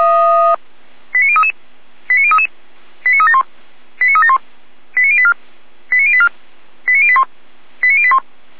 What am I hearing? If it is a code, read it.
Five-tone selective call (ZVEI), heard as quick runs of five tones, tones of about 70 ms. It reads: 89033 89033 89371 89371 89084 89084 89081 89081
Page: /455/SDR/ZVEI/89171/2026.04.14/